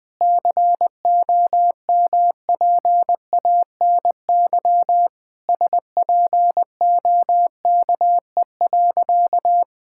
Morse code, heard as a keyed tone, about 21 wpm, 700 Hz. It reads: COMPANY SPOKE.
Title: Sentences with 3-Word Phrases - Spoken Beginning - 20wpm